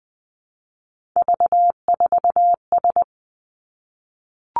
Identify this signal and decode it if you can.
Morse code, heard as a keyed tone, about 20 wpm, 700 Hz. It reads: V4S
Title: Three Challenging Letter-Number Combinations 20wpm